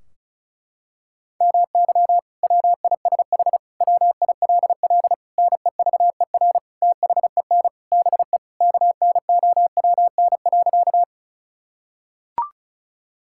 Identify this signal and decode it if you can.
Morse code, heard as a keyed tone, about 34 wpm, 700 Hz. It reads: MY WISH WILL NEVER THEN BE KNOWN.
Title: Sentences from Top 1600 Words 35wpm